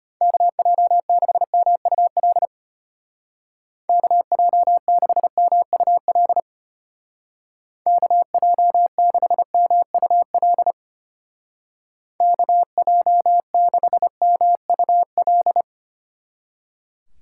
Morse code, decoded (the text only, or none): KJ6MUL KJ6MUL KJ6MUL KJ6MUL